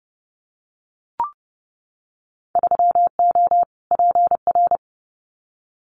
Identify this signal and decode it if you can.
Morse code, heard as a keyed tone, about 30 wpm, 700 Hz. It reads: 3O PR
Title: Winter Field Day Exchanges Encore 30wpm